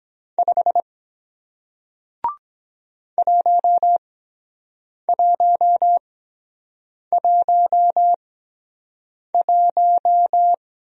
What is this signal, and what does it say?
Morse code, heard as a keyed tone, about 23 wpm, 700 Hz. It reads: 5 1 1 1 1